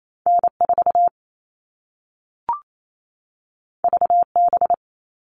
Morse code, heard as a keyed tone, about 28 wpm, 700 Hz. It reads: N4 VB
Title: All Cut Numbers - 2-Digit 28wpm